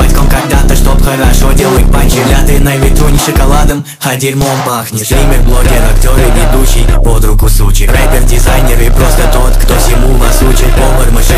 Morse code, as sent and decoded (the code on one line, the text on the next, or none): none